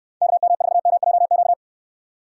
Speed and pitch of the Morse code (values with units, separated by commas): 63 wpm, 700 Hz